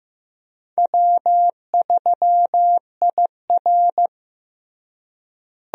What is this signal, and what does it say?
Morse code, heard as a keyed tone, about 15 wpm, 700 Hz. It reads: W3IR
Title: USA Call Signs - 4 Characters - Repeated 3x - 15wpm